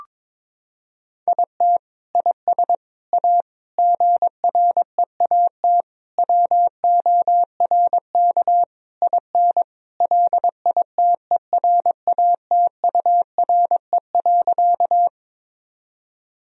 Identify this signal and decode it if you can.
Morse code, heard as a keyed tone, about 22 wpm, 700 Hz. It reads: IT IS A GREAT WORK IN LITERATURE.
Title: Sentences from Top 1000 Words - Review Part 3 - 22wpm